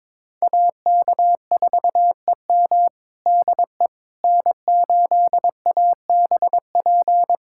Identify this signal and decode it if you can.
Morse code, heard as a keyed tone, about 21 wpm, 700 Hz. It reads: AK4EM DE N8ABP
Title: USA Callsign DE Callsign - 5 Characters - 22wpm